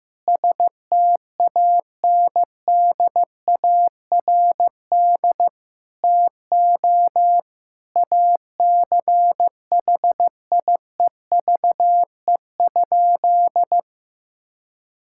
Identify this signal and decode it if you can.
Morse code, heard as a keyed tone, about 15 wpm, 700 Hz. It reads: STANDARD TO ACHIEVE?
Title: Sentences from Top 1000 Words - Review Part 6 - 15wpm